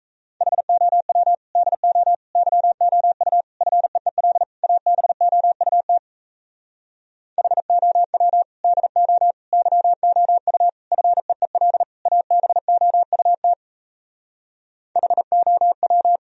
Morse code, heard as a keyed tone, about 38 wpm, 700 Hz. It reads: HOW DO YOU FEEL ABOUT HOW DO YOU FEEL ABOUT HOW